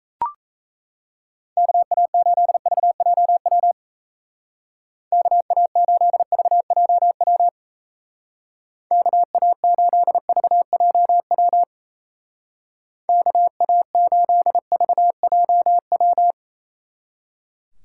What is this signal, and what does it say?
Morse code, heard as a keyed tone, about 33 wpm, 700 Hz. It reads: KA8VJW KA8VJW KA8VJW KA8VJW
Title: Speed Racing - USA Call Signs 28wpm